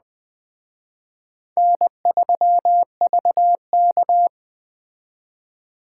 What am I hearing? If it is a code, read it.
Morse code, heard as a keyed tone, about 20 wpm, 700 Hz. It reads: N3VK
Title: Speed Racing - USA Call Signs 17wpm